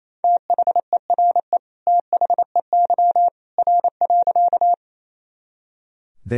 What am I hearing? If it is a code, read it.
Morse code, heard as a keyed tone, about 28 wpm, 700 Hz. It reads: THERE THEY R.
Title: Sentences from Top 100 Words and Abbreviations 28wpm